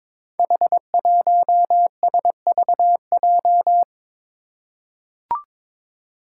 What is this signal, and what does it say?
Morse code, heard as a keyed tone, about 22 wpm, 700 Hz. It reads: H1SVJ
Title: Five Challenging Letter-Number Combinations 22wpm